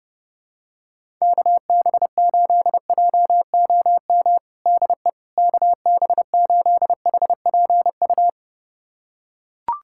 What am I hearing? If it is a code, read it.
Morse code, heard as a keyed tone, about 30 wpm, 700 Hz. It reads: KB8JOM DE KB8HPU
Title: USA Callsign DE Callsign - 6 Characters - 30wpm